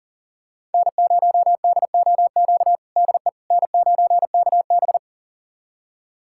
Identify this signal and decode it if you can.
Morse code, heard as a keyed tone, about 40 wpm, 700 Hz. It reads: N0DOQ DE N9KB
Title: USA Callsign DE Callsign 40wpm